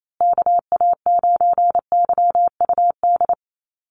Morse code, heard as a keyed tone, about 28 wpm, 700 Hz. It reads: KA9YUD
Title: USA Call Signs Repeated 3x 28wpm